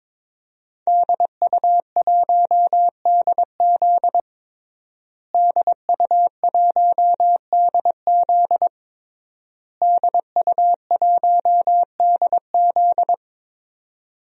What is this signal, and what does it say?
Morse code, heard as a keyed tone, about 22 wpm, 700 Hz. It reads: DU1DZ DU1DZ DU1DZ
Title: DXCC-100 Call Signs - 5 Characters - Repeated 3x - 22wpm